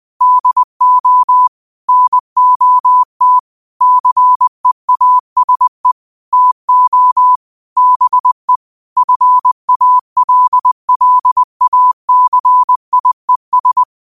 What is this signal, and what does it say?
Morse code, heard as a keyed tone, about 20 wpm, 1000 Hz. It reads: DO NOT CEASE TO BE FALLACIES
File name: qotdfiles / 20wpm